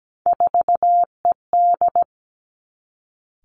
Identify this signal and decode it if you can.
Morse code, heard as a keyed tone, about 17 wpm, 700 Hz. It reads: 4ED
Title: Three Challenging Letter-Number Combinations 17wpm